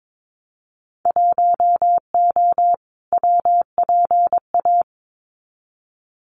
Morse code, heard as a keyed tone, about 22 wpm, 700 Hz. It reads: 1O WPA
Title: Winter Field Day Exchanges 22wpm